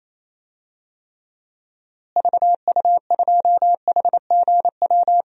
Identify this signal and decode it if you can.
Morse code, heard as a keyed tone, about 28 wpm, 700 Hz. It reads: VU2HGW
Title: DXCC-100 Call Signs - 6 Characters - Repeated 2x - 28wpm